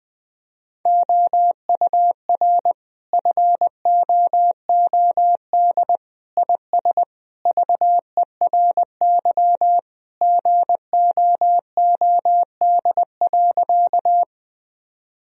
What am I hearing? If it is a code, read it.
Morse code, heard as a keyed tone, about 20 wpm, 700 Hz. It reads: OUR FOOD IS VERY GOOD.
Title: Sentences from Top 1000 Words - Review Part 1 - 20wpm